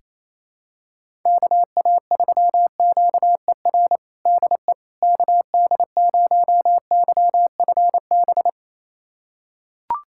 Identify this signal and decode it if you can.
Morse code, heard as a keyed tone, about 27 wpm, 700 Hz. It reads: KA3QER DE KD0YFB
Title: USA Callsign DE Callsign - 6 Characters - 28wpm